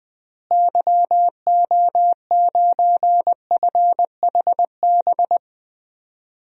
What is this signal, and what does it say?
Morse code, heard as a keyed tone, about 20 wpm, 700 Hz. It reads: YO9FHB
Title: DXCC-100 Call Signs - 6 Characters - Repeated 2x - 20wpm